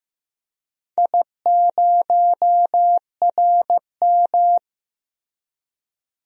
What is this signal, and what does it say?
Morse code, heard as a keyed tone, about 16 wpm, 700 Hz. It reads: I0RM
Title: DXCC-100 Call Signs - 4 Characters - Repeated 2x - 15wpm